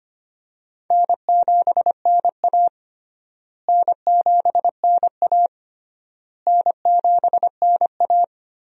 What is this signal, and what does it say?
Morse code, heard as a keyed tone, about 25 wpm, 700 Hz. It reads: N7NA N7NA N7NA
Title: USA Call Signs - 4 Characters - Repeated 3x - 25wpm